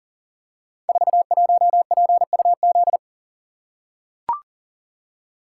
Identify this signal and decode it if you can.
Morse code, heard as a keyed tone, about 39 wpm, 700 Hz. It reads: 41PUZ
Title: Code Groups - 5 Letter:Numbers - 40wpm